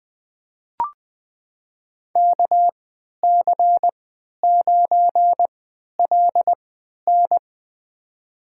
Morse code, heard as a keyed tone, about 20 wpm, spaced as for 11 wpm, 700 Hz. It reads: KC9LN
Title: USA Call Signs - 5 Characters - 3x Character Spacing - 20wpm